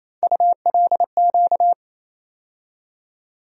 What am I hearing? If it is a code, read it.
Morse code, heard as a keyed tone, about 28 wpm, 700 Hz. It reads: ULQ
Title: USA Call Sign Postfixes - 3 Letters 28wpm